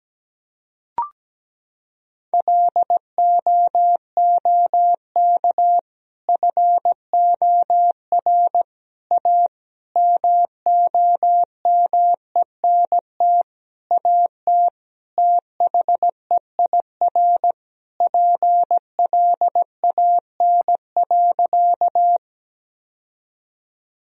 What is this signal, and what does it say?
Morse code, heard as a keyed tone, about 17 wpm, 700 Hz. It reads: LOOK FOR A MOMENT AT THEIR PLAN.
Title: Sentences from Top 1000 Words - Review Part 2 - 17wpm